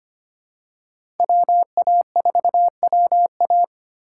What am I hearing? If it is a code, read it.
Morse code, heard as a keyed tone, about 25 wpm, 700 Hz. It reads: WA4WA
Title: USA Call Signs - 5 Characters - 3x Character Spacing - 25wpm